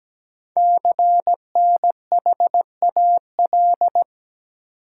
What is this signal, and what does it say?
Morse code, heard as a keyed tone, about 17 wpm, 700 Hz. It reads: CNHAL
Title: Code Groups - 5 Letters - 17wpm